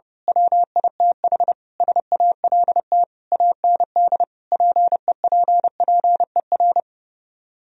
Morse code, heard as a keyed tone, about 28 wpm, 700 Hz. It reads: WITH SALT AND PEPPER
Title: Speed-Racing - Sets of 5 Words 30wpm